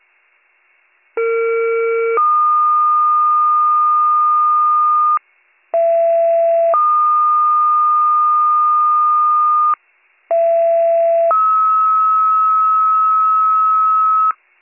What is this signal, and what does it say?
Two-tone page, heard as a one-second tone, then a three-second tone, about 460 then 1220 Hz, about 670 then 1220 Hz, about 670 then 1290 Hz. It is heard from 1.2 to 14.3 s.